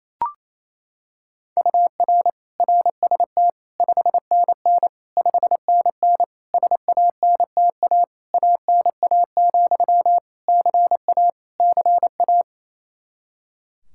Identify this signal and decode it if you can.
Morse code, heard as a keyed tone, about 28 wpm, 700 Hz. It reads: UR RST 5NN 5NN SANTA ANA, CA CA
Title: Practice QSO - Exchange 1 - Beginning - 28wpm